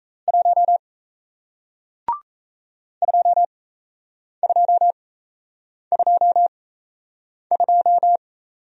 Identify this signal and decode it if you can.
Morse code, heard as a keyed tone, about 38 wpm, spaced as for 8 wpm, 700 Hz. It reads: 1 2222